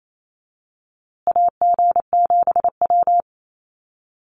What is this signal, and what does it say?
Morse code, heard as a keyed tone, about 28 wpm, 700 Hz. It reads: AG7W
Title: USA Call Signs Repeated 3x 28wpm